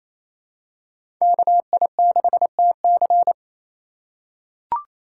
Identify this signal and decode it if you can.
Morse code, heard as a keyed tone, about 28 wpm, 700 Hz. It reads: KI6TC
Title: USA Call Signs - 2x Character Spacing - 28wpm